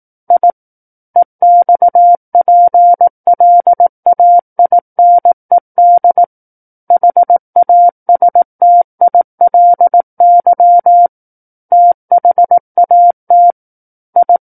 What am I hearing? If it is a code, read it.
Morse code, heard as a keyed tone, about 18 wpm, 700 Hz. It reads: I EXPLAINED HASTILY THAT I
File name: 18 WPM